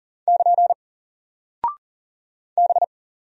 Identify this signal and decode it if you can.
Morse code, heard as a keyed tone, about 40 wpm, 700 Hz. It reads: <KN> B
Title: Single Character (Letter:Number:Punctuation:Prosign) Alt 40wpm